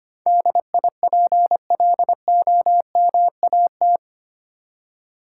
Morse code, heard as a keyed tone, about 25 wpm, 700 Hz. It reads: DIPLOMAT
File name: 8 Letter Words - First 3 Letters Given - 25wpm